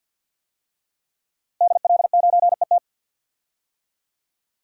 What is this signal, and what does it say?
Morse code, heard as a keyed tone, about 52 wpm, 700 Hz. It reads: DL9ET
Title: DXCC-100 Call Signs - 5 Characters - Repeated 3x - 50wpm